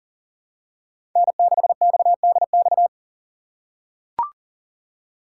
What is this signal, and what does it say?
Morse code, heard as a keyed tone, about 40 wpm, 700 Hz. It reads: N6XDX
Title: USA Call Signs - 2x Character Spacing - 40wpm